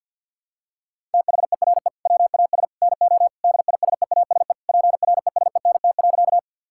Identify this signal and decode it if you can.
Morse code, heard as a keyed tone, about 50 wpm, 700 Hz. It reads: THERE WAS NO DISEASE PRESENT.
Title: Sentences from Top 1000 Words - Review Part 6 - 50wpm